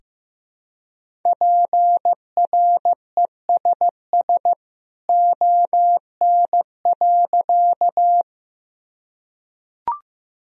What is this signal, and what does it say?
Morse code, heard as a keyed tone, about 15 wpm, 700 Hz. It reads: PRESS ON.
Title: Sentences from Top 1000 Words - Review Part 1 - 15wpm